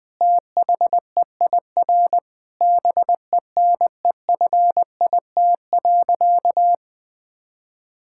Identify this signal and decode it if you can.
Morse code, heard as a keyed tone, about 20 wpm, 700 Hz. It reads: THEIR BENEFIT.
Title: Sentences from Top 1000 Words - Review Part 6 - 20wpm